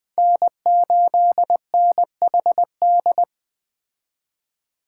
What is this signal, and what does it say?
Morse code, heard as a keyed tone, about 21 wpm, 700 Hz. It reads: N8NHD
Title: Speed Racing - USA 1x3 Call Signs 17wpm